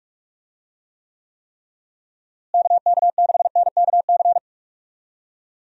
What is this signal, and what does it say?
Morse code, heard as a keyed tone, about 41 wpm, 700 Hz. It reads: KK6NKC
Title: USA Call Signs - 6 Characters - Repeated 3x - 45wpm